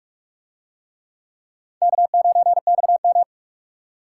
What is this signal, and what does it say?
Morse code, heard as a keyed tone, about 42 wpm, 700 Hz. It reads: K9XM